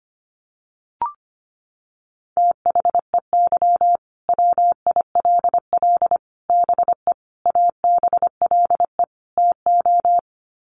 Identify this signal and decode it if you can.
Morse code, heard as a keyed tone, about 25 wpm, 690 Hz. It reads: THEY WILL BE ABLE TO
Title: Sets of 5 Words Encore 25wpm